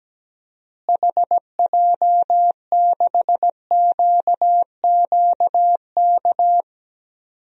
Morse code, heard as a keyed tone, about 17 wpm, 700 Hz. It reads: HJ6QQK
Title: DXCC-100 Call Signs - 6 Characters - Repeated 3x - 17wpm